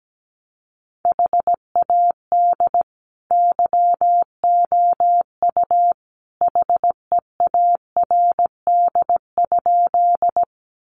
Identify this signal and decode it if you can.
Morse code, heard as a keyed tone, about 17 wpm, 700 Hz. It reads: HAD YOU HEARD?
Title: Sentences from Top 400 Words 17wpm